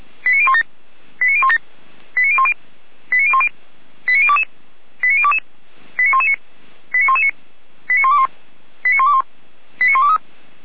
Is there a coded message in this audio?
Yes: five-tone selective call (ZVEI) from 0.3 to 10.2 s, heard as quick runs of five tones, tones of about 70 ms. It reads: 89017 89017 89010 89010 89022 89022 89119 89119 89121 89121 89123